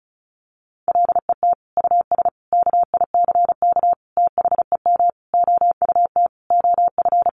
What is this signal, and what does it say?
Morse code, heard as a keyed tone, about 34 wpm, 700 Hz. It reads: LET US KICK THEM OUT OF
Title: Sentences from Top 1300 Words 35wpm